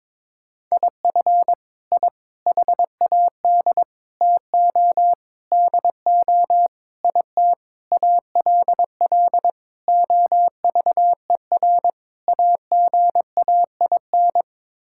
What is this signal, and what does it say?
Morse code, heard as a keyed tone, about 22 wpm, 700 Hz. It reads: IF I HAD TO DO IT ALL OVER AGAIN
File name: Sets of 9 Words - 22wpm